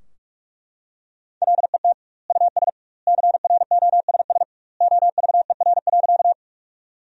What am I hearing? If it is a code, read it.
Morse code, heard as a keyed tone, about 42 wpm, 700 Hz. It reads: LET US CROSS OVER.
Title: 2x Word-Spacing - Sentences from Top 300 Words - 45wpm